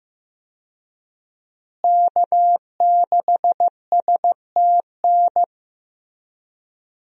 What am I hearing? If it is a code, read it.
Morse code, heard as a keyed tone, about 15 wpm, 700 Hz. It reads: K6STN
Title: USA Call Signs - 5 Characters - Repeated 2x - 15wpm